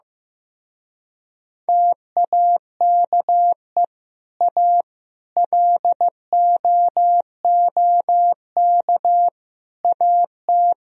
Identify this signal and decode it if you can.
Morse code, heard as a keyed tone, about 15 wpm, 700 Hz. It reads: TAKE A LOOK AT
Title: Speed-Racing - Sets of 4 Words 15wpm